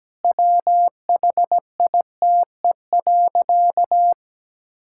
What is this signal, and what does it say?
Morse code, heard as a keyed tone, about 17 wpm, 700 Hz. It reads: WHITE.
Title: Pangram Challenge v2 - 17wpm